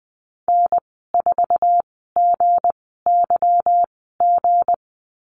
Morse code, heard as a keyed tone, about 20 wpm, spaced as for 14 wpm, 700 Hz. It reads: N4GYG
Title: 161 Lesson 26g - Letter g - Call Signs 20wpm